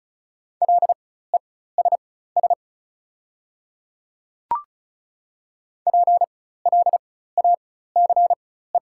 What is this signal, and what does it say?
Morse code, heard as a keyed tone, about 35 wpm, spaced as for 16 wpm, 690 Hz. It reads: LESS PLACE
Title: 4x Character Spacing - Top 500 Words - 35wpm